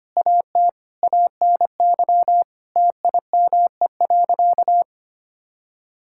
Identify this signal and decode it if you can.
Morse code, heard as a keyed tone, about 25 wpm, 700 Hz. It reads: AT ANY TIME.
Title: Sentences from Top 1000 Words - Review Part 6 - 25wpm